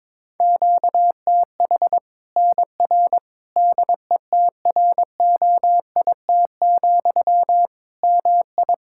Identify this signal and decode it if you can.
Morse code, heard as a keyed tone, about 22 wpm, 700 Hz. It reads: QTH NR DETROIT, MI
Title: Practice QSO - Exchange 1 - Beginning - 22wpm